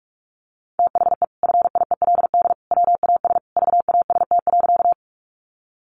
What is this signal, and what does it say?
Morse code, heard as a keyed tone, about 44 wpm, 700 Hz. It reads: THE FIELD WAS VAST.
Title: Sentences from Top 1300 Words 45wpm